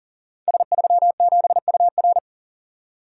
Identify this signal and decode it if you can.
Morse code, heard as a keyed tone, about 42 wpm, 700 Hz. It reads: S37UR
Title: Code Groups - 5 Letter:Numbers - 40wpm